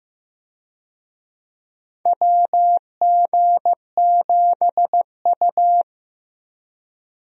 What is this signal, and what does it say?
Morse code, heard as a keyed tone, about 15 wpm, 700 Hz. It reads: WG7U
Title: USA Call Signs - 4 Characters - Repeated 3x - 15wpm